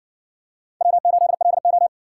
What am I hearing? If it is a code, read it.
Morse code, heard as a keyed tone, about 60 wpm, 700 Hz. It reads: W8LO